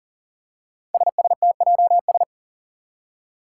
Morse code, heard as a keyed tone, about 40 wpm, 700 Hz. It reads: SSTJS
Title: Five Challenging Letter-Number Combinations 40wpm